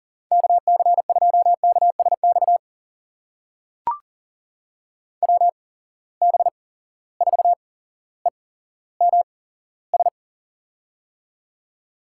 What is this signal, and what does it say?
Morse code, heard as a keyed tone, about 40 wpm, spaced as for 17 wpm, 700 Hz. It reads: KC2KSX WB4EMS
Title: USA Call Signs - 6 Characters - 8x Character Spacing - 40wpm